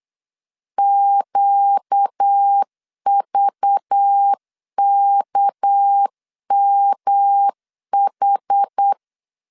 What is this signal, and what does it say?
Morse code, heard as a keyed tone, about 8 wpm, 800 Hz. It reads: QVKMH
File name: LEZIONI / 6 PKX0 / 6 PKX0 8wpm 3341